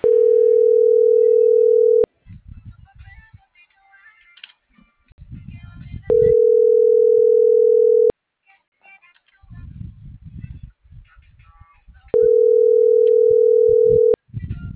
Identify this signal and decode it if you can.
Ringback tone (North America), 440 + 480 Hz.